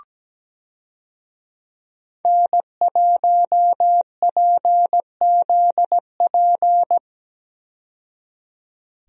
Morse code, heard as a keyed tone, about 17 wpm, 700 Hz. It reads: N1PZP
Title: USA Call Signs - 5 Characters - 17wpm